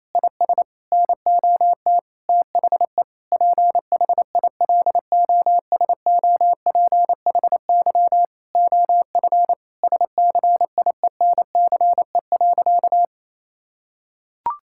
Morse code, heard as a keyed tone, about 27 wpm, 700 Hz. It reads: IS NOT THE PHILOSOPHY OF SCIENCE.
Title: Sentences from Top 1000 Words - Review Part 6 - 28wpm